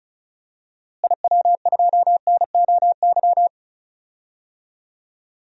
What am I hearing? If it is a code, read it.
Morse code, heard as a keyed tone, about 36 wpm, 700 Hz. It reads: IW2NOY
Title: DXCC-100 Call Signs - 6 Characters - Repeated 2x - 35wpm